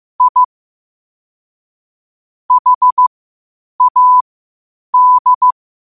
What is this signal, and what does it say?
Morse code, heard as a keyed tone, about 15 wpm, spaced as for 8 wpm, 1000 Hz. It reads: I HAD